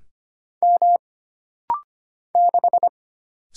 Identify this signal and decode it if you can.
Morse code, heard as a keyed tone, about 25 wpm, 700 Hz. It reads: M 6
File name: Single Character (Letter:Number:Punctuation:Prosign) - Rapid-Fire - 25wpm